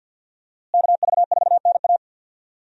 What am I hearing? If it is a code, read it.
Morse code, heard as a keyed tone, about 48 wpm, 700 Hz. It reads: KV4NA
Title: USA Call Signs - 5 Characters - Repeated 2x - 50wpm